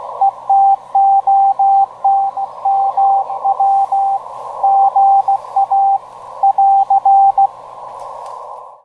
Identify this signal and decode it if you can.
Morse code, heard as a keyed tone, about 16 wpm, 790 Hz, with radio noise. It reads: ETOT MW GA <AR>
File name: tracks / kb2seo eton ga 28.238